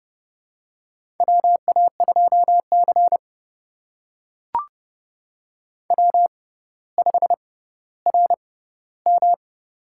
Morse code, heard as a keyed tone, about 29 wpm, spaced as for 15 wpm, 700 Hz. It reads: WA2C W5RM
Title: USA Call Signs - 4 Characters - 6x Character Spacing - 30wpm